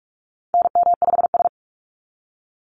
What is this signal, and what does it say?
Morse code, heard as a keyed tone, about 46 wpm, 700 Hz. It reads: NM5S
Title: NA SOTA Chaser Callsigns 45wpm